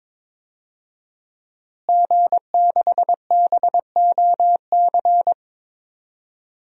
Morse code, heard as a keyed tone, about 21 wpm, 700 Hz. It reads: G6BOC
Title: DXCC-100 Call Signs - 5 Characters - Repeated 3x - 22wpm